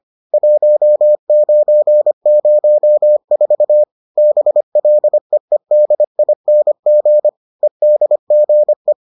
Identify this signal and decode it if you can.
Morse code, heard as a keyed tone, about 25 wpm, 600 Hz. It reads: 1904 BLEEDING EDGE